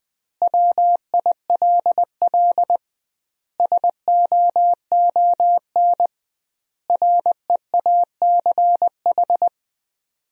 Morse code, heard as a keyed tone, about 19 wpm, 700 Hz. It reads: WILL SOON REACH
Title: 2x Word-Spacing - Sentences from Top 400 Words - 20wpm